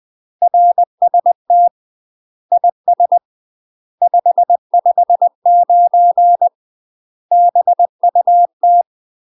Morse code, heard as a keyed tone, about 20 wpm, 700 Hz. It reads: RST IS 559 BUT
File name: UK MASTER QSOs TRAINING/20WPM